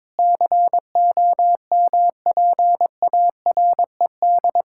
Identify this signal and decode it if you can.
Morse code, heard as a keyed tone, about 22 wpm, 700 Hz. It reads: COMPARED
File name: practice / 22wpm / Sentences with 8-Word Phrases - 22wpm